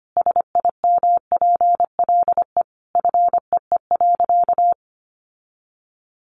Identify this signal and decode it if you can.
Morse code, heard as a keyed tone, about 25 wpm, 700 Hz. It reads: SIMPLE FEE.
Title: Sentences from Top 1500 Words 25wpm